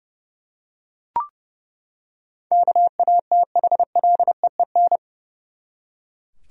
Morse code, heard as a keyed tone, about 30 wpm, 700 Hz. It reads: KATHLEEN
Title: USA Common Names - 30wpm